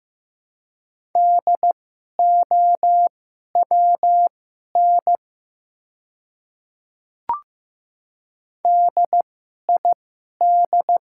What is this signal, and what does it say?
Morse code, heard as a keyed tone, about 15 wpm, spaced as for 11 wpm, 700 Hz. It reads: DOWN DID
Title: 2x Character Spacing - Top 200 Words - 15wpm